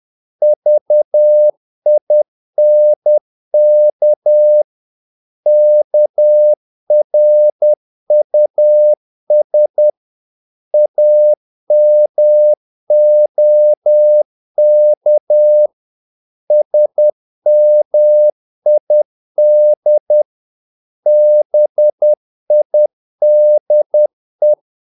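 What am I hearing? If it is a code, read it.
Morse code, heard as a keyed tone, about 10 wpm, 600 Hz. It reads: VINK KRUS AMOK SMID BIDE
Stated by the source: Morse practice of short Danish words at 10 wpm, from CW med Gnister.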